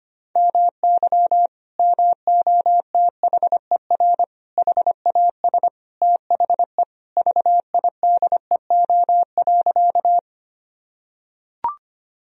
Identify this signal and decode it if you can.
Morse code, heard as a keyed tone, about 25 wpm, 700 Hz. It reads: MY MOTHER HAS THE VIDEO.
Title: Sentences from Top 1000 Words - Review Part 4 - 25wpm